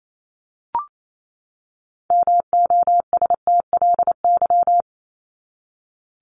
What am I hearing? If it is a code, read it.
Morse code, heard as a keyed tone, about 28 wpm, 700 Hz. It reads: MOSTLY
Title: Top 1001-1250 Words 28wpm